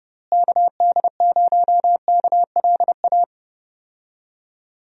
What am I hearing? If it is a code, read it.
Morse code, heard as a keyed tone, about 30 wpm, 700 Hz. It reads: KD0KLA
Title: USA Call Signs - 6 Characters - Repeated 3x - 30wpm